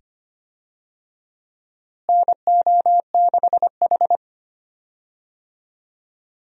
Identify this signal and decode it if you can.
Morse code, heard as a keyed tone, about 25 wpm, 700 Hz. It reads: NO6H
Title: USA Call Signs - 4 Characters - Repeated 2x - 25wpm